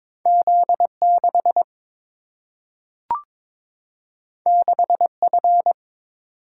Morse code, heard as a keyed tone, about 22 wpm, 700 Hz. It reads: Z6 6F
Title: Single-Letter Single-Number 22wpm